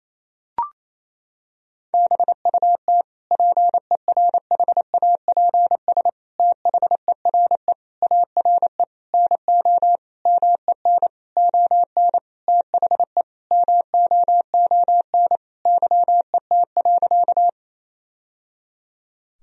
Morse code, heard as a keyed tone, about 28 wpm, 700 Hz. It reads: BUT PERHAPS THERE ARE NO MEN ON THE MOON YET.
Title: Sentences from Top 1000 Words - Review Part 3 - 28wpm